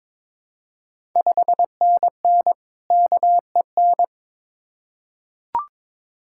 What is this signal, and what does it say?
Morse code, heard as a keyed tone, about 22 wpm, 690 Hz. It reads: 5NN KEN
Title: Ontario QSO Party Exchange 22wpm